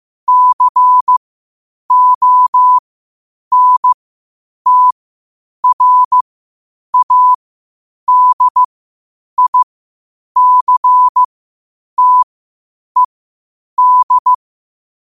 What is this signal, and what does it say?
Morse code, heard as a keyed tone, about 15 wpm, spaced as for 8 wpm, 1000 Hz. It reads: CONTRADICTED
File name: qotdfiles / 8wpm